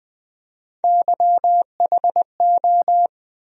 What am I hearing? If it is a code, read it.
Morse code, heard as a keyed tone, about 21 wpm, 700 Hz. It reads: YHO
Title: USA Call Sign Postfixes - 3 Letters 20wpm